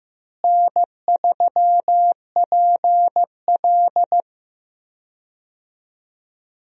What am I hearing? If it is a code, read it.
Morse code, heard as a keyed tone, about 15 wpm, 700 Hz. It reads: N3PL
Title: USA Call Signs - 4 Characters - Repeated 3x - 15wpm